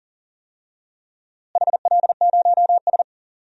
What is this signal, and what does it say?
Morse code, heard as a keyed tone, about 40 wpm, 700 Hz. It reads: HL0S
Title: DXCC-100 Call Signs - 4 Characters - Repeated 2x - 40wpm